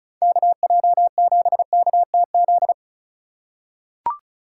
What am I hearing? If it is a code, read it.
Morse code, heard as a keyed tone, about 36 wpm, 700 Hz. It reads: KJ7KTZ
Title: USA Call Signs - 6 Characters - 3x Character Spacing - 35wpm